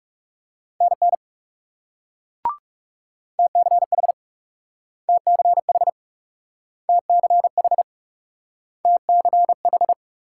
Morse code, heard as a keyed tone, about 38 wpm, 700 Hz. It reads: NN TCH TCH TCH TCH